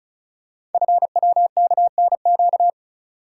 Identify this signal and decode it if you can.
Morse code, heard as a keyed tone, about 36 wpm, 700 Hz. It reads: FWKNQ
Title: Code Groups - 5 Letters - 35wpm